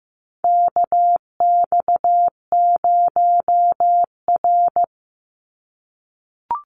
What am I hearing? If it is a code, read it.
Morse code, heard as a keyed tone, about 15 wpm, 700 Hz. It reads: KX0R
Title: NA SOTA Activator Callsigns 15wpm